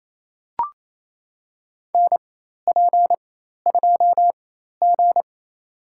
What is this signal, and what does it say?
Morse code, heard as a keyed tone, about 28 wpm, spaced as for 13 wpm, 700 Hz. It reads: NP2G
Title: USA Call Signs - 4 Characters - 4x Character Spacing - 28wpm